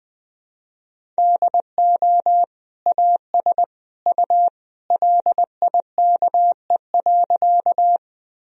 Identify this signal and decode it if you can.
Morse code, heard as a keyed tone, about 20 wpm, 700 Hz. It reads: DO AS U LIKE.
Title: Sentences from Top 100 Words and Abbreviations 20wpm